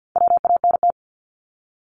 Morse code, heard as a keyed tone, about 50 wpm, 700 Hz. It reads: WANT
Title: Multi-letter Phonemes Repeated 3x 50wpm